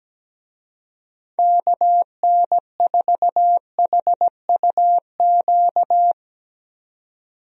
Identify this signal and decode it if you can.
Morse code, heard as a keyed tone, about 17 wpm, 700 Hz. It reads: KN4HUQ
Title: USA Call Signs - 6 Characters - 17wpm